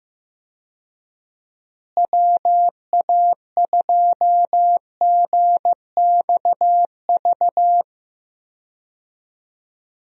Morse code, heard as a keyed tone, about 15 wpm, 700 Hz. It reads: WA2GXV
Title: USA Call Signs - 6 Characters - Repeated 2x - 15wpm